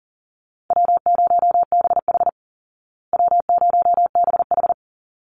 Morse code, heard as a keyed tone, about 40 wpm, 700 Hz. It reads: W0BH W0BH
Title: Easy Call Signs Repeated 2x - 40wpm